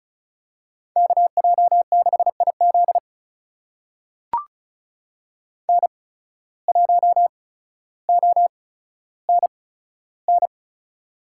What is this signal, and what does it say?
Morse code, heard as a keyed tone, about 36 wpm, spaced as for 15 wpm, 700 Hz. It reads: KJ6IZ N1ONN
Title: USA Call Signs - 5 Characters - 8x Character Spacing - 35wpm